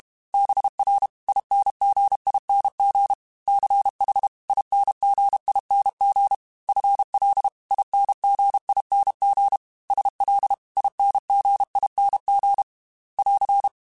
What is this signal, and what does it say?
Morse code, heard as a keyed tone, about 32 wpm, 800 Hz. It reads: BR INGING CH INGING FL INGING SL INGING <AR>
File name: Special INGs-mp3s / Special INGs-mp3s / Dble Ings sp @30